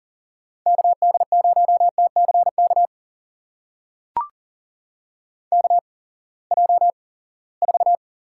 Morse code, heard as a keyed tone, about 40 wpm, spaced as for 23 wpm, 700 Hz. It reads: KD0TCK KJ4
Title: USA Call Signs - 6 Characters - 8x Character Spacing - 40wpm